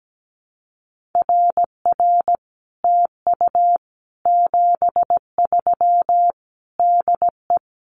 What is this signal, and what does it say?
Morse code, heard as a keyed tone, about 17 wpm, 700 Hz. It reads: RR TU 73 DE
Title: Common QSO Phrases Encore 17wpm